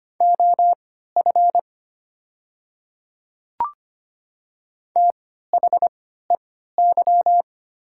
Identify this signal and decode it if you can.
Morse code, heard as a keyed tone, about 25 wpm, spaced as for 14 wpm, 700 Hz. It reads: OF THEY
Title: 3x Character Spacing - Top 200 Words - 25wpm